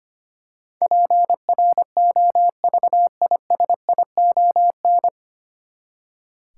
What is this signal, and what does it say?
Morse code, heard as a keyed tone, about 25 wpm, 700 Hz. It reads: PROVISION
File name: Nine Letter Words - 25wpm